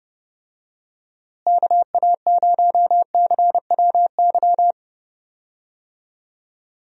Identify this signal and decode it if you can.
Morse code, heard as a keyed tone, about 30 wpm, 700 Hz. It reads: KA0CWY
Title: USA Call Signs - 6 Characters - Repeated 3x - 30wpm